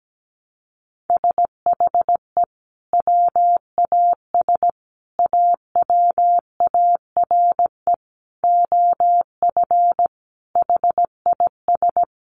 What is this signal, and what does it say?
Morse code, heard as a keyed tone, about 17 wpm, 700 Hz. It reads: SHE WAS AWARE OF HIS
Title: Sentences from Top 1200 Words 17wpm